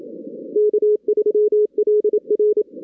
Morse code, heard as a keyed tone, about 27 wpm, 410 Hz. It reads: K3LR
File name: 2016 arrl dx cw / 2016-ARRL-DX-CW-K5TR- - 0-22